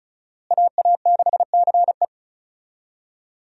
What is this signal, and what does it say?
Morse code, heard as a keyed tone, about 34 wpm, 700 Hz. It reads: AA6CE
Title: USA Call Signs - 5 Characters - Repeated 3x - 35wpm